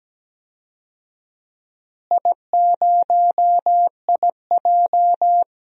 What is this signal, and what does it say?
Morse code, heard as a keyed tone, about 17 wpm, 700 Hz. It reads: I0IJ
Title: DXCC-100 Call Signs - 4 Characters - Repeated 2x - 17wpm